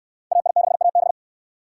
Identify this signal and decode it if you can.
Morse code, heard as a keyed tone, about 59 wpm, 700 Hz. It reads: AI6AB